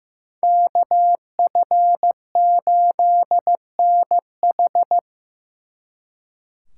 Morse code, heard as a keyed tone, about 18 wpm, 700 Hz. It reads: KF8NH
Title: Speed Racing - USA Call Signs 15wpm